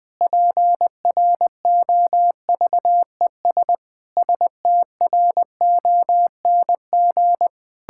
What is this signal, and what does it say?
Morse code, heard as a keyed tone, about 20 wpm, 700 Hz. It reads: PROVES STRONG
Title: Pangram Challenge v2 - 20wpm